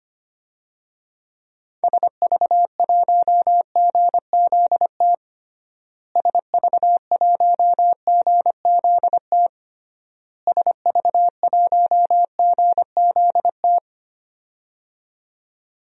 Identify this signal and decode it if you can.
Morse code, heard as a keyed tone, about 25 wpm, 700 Hz. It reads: SV1GZT SV1GZT SV1GZT
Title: DXCC-100 Call Signs - 6 Characters - Repeated 3x - 25wpm